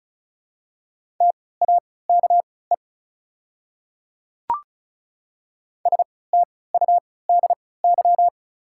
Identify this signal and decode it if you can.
Morse code, heard as a keyed tone, about 34 wpm, spaced as for 20 wpm, 700 Hz. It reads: TAKE STUDY
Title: 3x Character Spacing - Top 300 Words - 35wpm